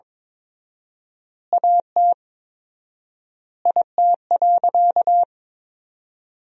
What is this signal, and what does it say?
Morse code, heard as a keyed tone, about 22 wpm, 700 Hz. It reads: AT IT.
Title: 4x Word-Spacing - Sentences from Top 300 Words - 22wpm